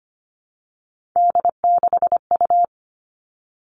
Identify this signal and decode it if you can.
Morse code, heard as a keyed tone, about 25 wpm, 700 Hz. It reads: D6U
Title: All Cut Numbers - 3-Digit 25wpm